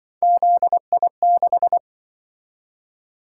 Morse code, heard as a keyed tone, about 26 wpm, 700 Hz. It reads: ZI6